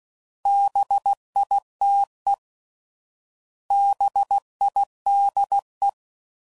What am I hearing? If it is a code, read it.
Morse code, heard as a keyed tone, about 16 wpm, 800 Hz. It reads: BITE BIDE